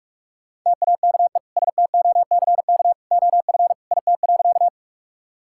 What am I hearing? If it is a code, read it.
Morse code, heard as a keyed tone, about 42 wpm, 700 Hz. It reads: TAKE STOCK OF IT.
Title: Sentences from Top 1000 Words - Review Part 5 - 45wpm